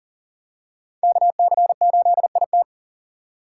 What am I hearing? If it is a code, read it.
Morse code, heard as a keyed tone, about 40 wpm, 700 Hz. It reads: KC8IT
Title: USA Call Signs - 4x Character Spacing - 40wpm